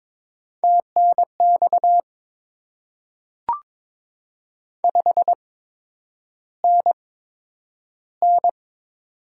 Morse code, heard as a keyed tone, about 23 wpm, spaced as for 10 wpm, 700 Hz. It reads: TNX 5NN
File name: Top 20 QSO Elements - 8x Character Spacing - 22wpm